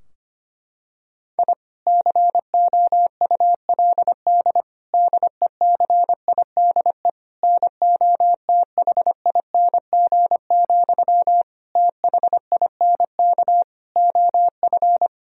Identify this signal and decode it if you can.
Morse code, heard as a keyed tone, about 25 wpm, 700 Hz. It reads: I COULD DECIDE NOTHING, THINK OF
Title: Sentences from Top 1000 Words - Review Part 3 - 25wpm